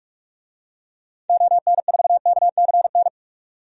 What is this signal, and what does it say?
Morse code, heard as a keyed tone, about 43 wpm, 700 Hz. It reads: ON4KCN
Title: DXCC-100 Call Signs - 6 Characters - 45wpm